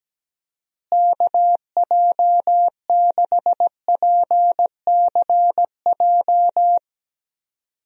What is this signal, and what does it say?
Morse code, heard as a keyed tone, about 17 wpm, 700 Hz. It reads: KJ6PCJ
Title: USA Call Signs - 6 Characters - Repeated 2x - 17wpm